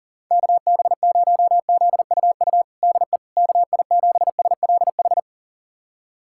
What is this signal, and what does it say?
Morse code, heard as a keyed tone, about 40 wpm, 700 Hz. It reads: KB0ZUU DE KI7SLH
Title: USA Callsign DE Callsign - 6 Characters - 40wpm